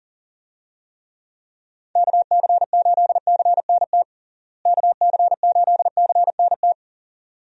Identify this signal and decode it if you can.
Morse code, heard as a keyed tone, about 40 wpm, 700 Hz. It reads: KC8CNT KC8CNT
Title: USA Call Signs - 6 Characters - Repeated 2x - 40wpm